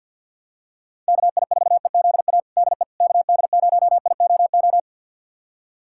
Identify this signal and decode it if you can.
Morse code, heard as a keyed tone, about 50 wpm, 700 Hz. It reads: KI4EZA DE KD0IOO
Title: USA Callsign DE Callsign - 6 Characters - 50wpm